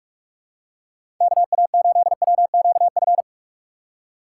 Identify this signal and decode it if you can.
Morse code, heard as a keyed tone, about 41 wpm, 700 Hz. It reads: KA8WQF
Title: USA Call Signs - 6 Characters - 2x Character Spacing - 45wpm